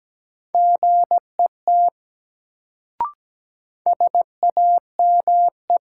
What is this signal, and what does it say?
Morse code, heard as a keyed tone, about 17 wpm, 700 Hz. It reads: GET SAME
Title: Sentences from Top 200 Words - New Word Drill - 17wpm